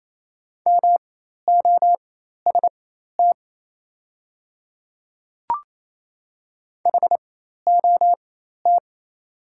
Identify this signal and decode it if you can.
Morse code, heard as a keyed tone, about 28 wpm, spaced as for 13 wpm, 700 Hz. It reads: MOST HOT
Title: 4x Character Spacing - Top 100 Words - 28wpm